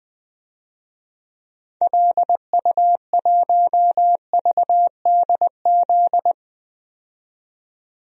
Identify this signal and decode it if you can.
Morse code, heard as a keyed tone, about 20 wpm, 700 Hz. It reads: LU1VDZ
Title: DXCC-100 Call Signs - 6 Characters - 20wpm